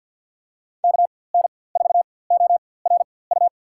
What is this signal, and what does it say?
Morse code, heard as a keyed tone, about 51 wpm, spaced as for 23 wpm, 700 Hz. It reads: KN4ORU
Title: USA Call Signs - 6 Characters - 4x Character Spacing - 50wpm